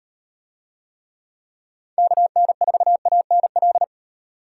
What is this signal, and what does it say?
Morse code, heard as a keyed tone, about 35 wpm, 700 Hz. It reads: KN4ANL